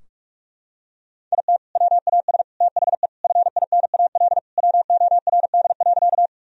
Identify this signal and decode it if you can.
Morse code, heard as a keyed tone, about 41 wpm, 700 Hz. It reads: IT WAS THE FINAL WORD.
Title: Sentences from Top 1000 Words - Review Part 2 - 45wpm